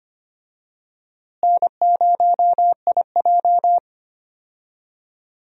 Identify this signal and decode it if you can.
Morse code, heard as a keyed tone, about 25 wpm, 700 Hz. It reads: N0IJ
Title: USA Call Signs - 4 Characters - Repeated 3x - 25wpm